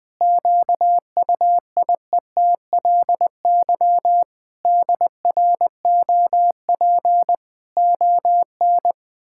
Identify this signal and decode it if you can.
Morse code, heard as a keyed tone, about 20 wpm, 700 Hz. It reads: QUIETLY DROP ON
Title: Pangram Challenge v2 - 20wpm